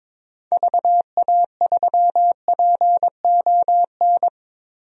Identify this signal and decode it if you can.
Morse code, heard as a keyed tone, about 22 wpm, 700 Hz. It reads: VA3PON
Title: DXCC-100 Call Signs - 6 Characters - Repeated 3x - 22wpm